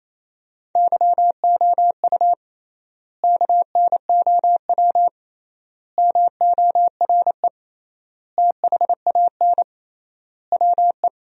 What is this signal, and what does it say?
Morse code, heard as a keyed tone, about 28 wpm, 700 Hz. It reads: YOU KNOW MORE THAN WE
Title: 3x Word-Spacing - Sentences from Top 100 Words - 28wpm